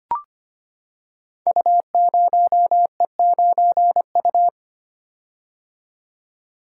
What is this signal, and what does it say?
Morse code, heard as a keyed tone, about 25 wpm, 700 Hz. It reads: U0E9U
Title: Code Groups - 5 Letter:Numbers - 25wpm